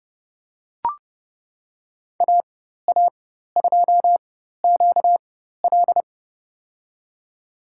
Morse code, heard as a keyed tone, about 30 wpm, spaced as for 14 wpm, 700 Hz. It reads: AA2QL
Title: USA Call Signs - 5 Characters - 4x Character Spacing - 30wpm